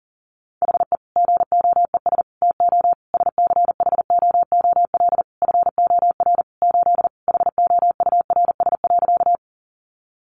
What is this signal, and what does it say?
Morse code, heard as a keyed tone, about 40 wpm, 700 Hz. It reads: HE GOES TO SCHOOL FOR 8 HOURS.
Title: Additional Easy Phrases 40wpm